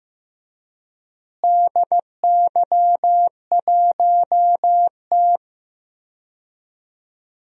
Morse code, heard as a keyed tone, about 15 wpm, 700 Hz. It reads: DY1T
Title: DXCC-100 Call Signs - 4 Characters - Repeated 3x - 15wpm